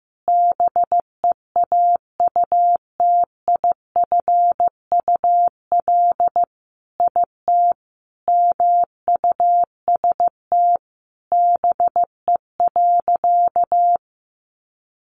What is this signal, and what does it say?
Morse code, heard as a keyed tone, about 15 wpm, 700 Hz. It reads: BEAUTIFUL IT MUST BE.
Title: Sentences from Top 900 Words 15wpm